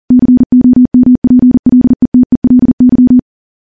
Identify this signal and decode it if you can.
Morse code, heard as a keyed tone, about 40 wpm, 250 Hz. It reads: COMPLETELY
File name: morse / morse-training-40WPM